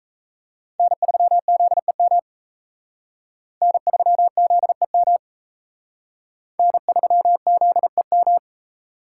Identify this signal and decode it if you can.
Morse code, heard as a keyed tone, about 34 wpm, 700 Hz. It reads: N3ZEM N3ZEM N3ZEM
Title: Speed Racing - USA 1x3 Call Signs 28wpm